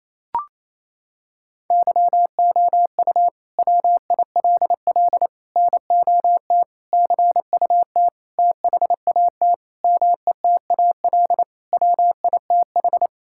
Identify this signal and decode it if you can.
Morse code, heard as a keyed tone, about 28 wpm, 700 Hz. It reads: YOU WILL NOT CUT THAT METAL WITH
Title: Sentences from Top 1000 Words - Review Part 6 - 28wpm